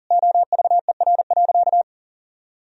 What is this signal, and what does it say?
Morse code, heard as a keyed tone, about 40 wpm, 700 Hz. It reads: OVER.
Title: Sentences from Top 1000 Words - Review Part 1 - 40wpm